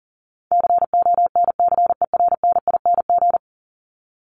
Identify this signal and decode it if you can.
Morse code, heard as a keyed tone, about 40 wpm, 700 Hz. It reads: CONCERNING
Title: ING Suffix - Top 100 Words with Base Words 40wpm